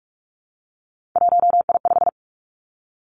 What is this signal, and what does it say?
Morse code, heard as a keyed tone, about 46 wpm, 700 Hz. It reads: 1I5
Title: Three Challenging Letter-Number Combinations 45wpm